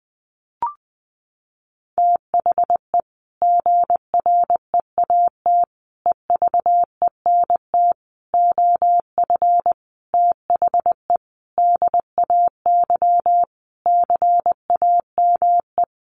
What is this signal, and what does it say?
Morse code, heard as a keyed tone, about 20 wpm, 700 Hz. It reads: THE GREAT EVENT OF THE DAY CAME
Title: Sentences from Top 1500 Words 20wpm